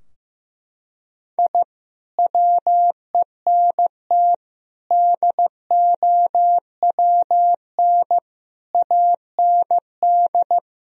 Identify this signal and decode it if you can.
Morse code, heard as a keyed tone, about 15 wpm, 700 Hz. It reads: I WENT DOWN AND
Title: Sentences from Top 1000 Words - Review Part 2 - 15wpm